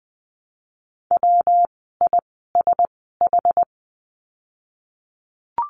Morse code, heard as a keyed tone, about 20 wpm, spaced as for 14 wpm, 700 Hz. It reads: WISH
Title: 115 Lesson 19f - Letter w - Words 20wpm